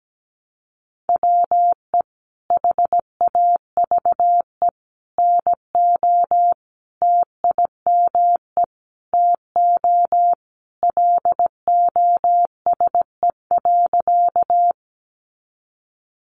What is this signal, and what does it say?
Morse code, heard as a keyed tone, about 17 wpm, 700 Hz. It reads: WE HAVE NO TIME TO LOSE.
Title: Sentences from Top 1100 Words 17wpm